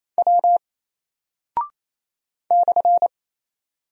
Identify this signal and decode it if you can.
Morse code, heard as a keyed tone, about 27 wpm, 700 Hz. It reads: W /
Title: Single Character (Letter:Number:Punctuation:Prosign) Alt 28wpm